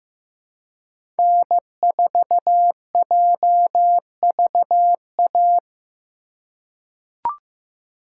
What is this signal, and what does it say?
Morse code, heard as a keyed tone, about 15 wpm, 700 Hz. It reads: N4JVA
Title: USA Call Signs - 5 Characters - 3x Character Spacing - 15wpm